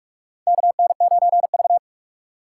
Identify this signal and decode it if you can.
Morse code, heard as a keyed tone, about 44 wpm, 690 Hz. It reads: KN9V
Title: USA Call Signs - 4 Characters - 3x Character Spacing - 45wpm